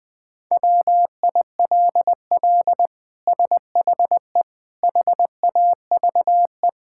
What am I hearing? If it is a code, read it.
Morse code, heard as a keyed tone, about 20 wpm, 700 Hz. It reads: WILL SHE HAVE
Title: Sentences from Top 1000 Words - Review Part 6 - 20wpm